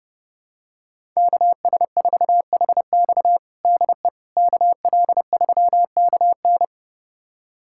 Morse code, heard as a keyed tone, about 32 wpm, 700 Hz. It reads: KS4HX DE KL3KN
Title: USA Callsign DE Callsign - 5 Characters - 30wpm